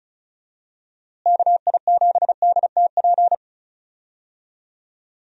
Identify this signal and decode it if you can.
Morse code, heard as a keyed tone, about 34 wpm, 700 Hz. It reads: KI7DTP
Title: USA Call Signs - 6 Characters - Repeated 3x - 35wpm